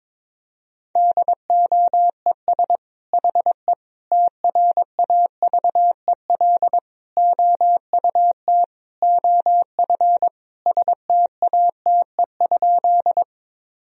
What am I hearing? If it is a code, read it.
Morse code, heard as a keyed tone, about 22 wpm, 700 Hz. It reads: DOES HE TRAVEL OUT OF STATE?
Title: Sentences from Top 1000 Words - Review Part 2 - 22wpm